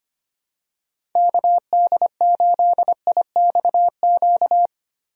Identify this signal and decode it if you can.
Morse code, heard as a keyed tone, about 25 wpm, 700 Hz. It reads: KD8IXQ
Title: USA Call Signs - 6 Characters - Repeated 2x - 25wpm